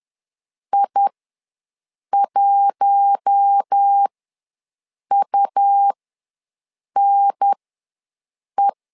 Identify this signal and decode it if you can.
Morse code, heard as a keyed tone, about 10 wpm, spaced as for 6 wpm, 800 Hz. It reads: I1UNE
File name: LEZIONI / 4 Q1 ws 10wpm 2538